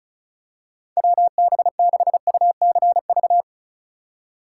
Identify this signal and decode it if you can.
Morse code, heard as a keyed tone, about 34 wpm, 700 Hz. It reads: WB6UCV
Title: USA Call Signs - 6 Characters - 2x Character Spacing - 35wpm